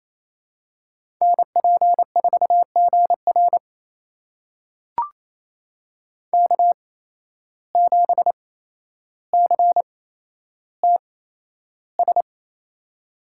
Morse code, heard as a keyed tone, about 28 wpm, spaced as for 12 wpm, 700 Hz. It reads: NP4GR K7CTS
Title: USA Call Signs - 5 Characters - 8x Character Spacing - 28wpm